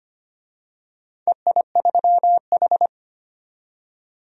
Morse code, heard as a keyed tone, about 25 wpm, 700 Hz. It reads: EI3H
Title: DXCC-100 Call Signs - 4 Characters - 25wpm